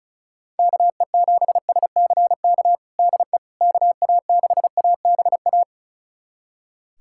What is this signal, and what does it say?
Morse code, heard as a keyed tone, about 36 wpm, 700 Hz. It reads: KE7SCK DE KA6ABA
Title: USA Callsign DE Callsign - 6 Characters - 35wpm